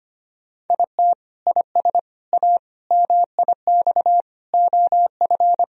Morse code, heard as a keyed tone, about 25 wpm, 700 Hz. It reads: IT IS A MIX OF
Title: Sentences from Top 1000 Words - Review Part 4 - 25wpm